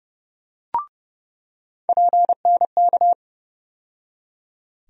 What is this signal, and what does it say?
Morse code, heard as a keyed tone, about 30 wpm, 700 Hz. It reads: PNK
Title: USA Call Sign Postfixes - 3 Letters 30wpm